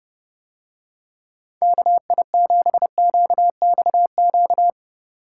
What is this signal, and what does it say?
Morse code, heard as a keyed tone, about 29 wpm, 700 Hz. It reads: KI7QXQ